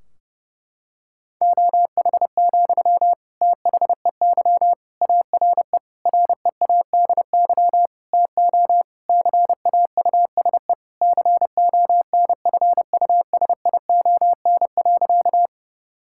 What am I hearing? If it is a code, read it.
Morse code, heard as a keyed tone, about 30 wpm, 700 Hz. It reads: OH, THEY ARE READY TO CAUSE CONFUSION.
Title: Sentences from Top 1600 Words 30wpm